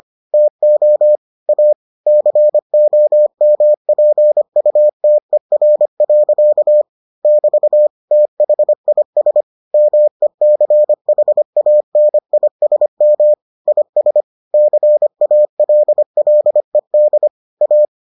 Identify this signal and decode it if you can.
Morse code, heard as a keyed tone, about 25 wpm, 600 Hz. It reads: TO A COMPUTER. <BT> THIS MECHANISM IS CALLED A